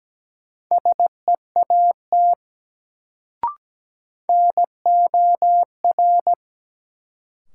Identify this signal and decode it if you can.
Morse code, heard as a keyed tone, about 17 wpm, 700 Hz. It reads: SEAT NOR
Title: Sentences from Top 1500 Words - New Word Drill Alt - 17wpm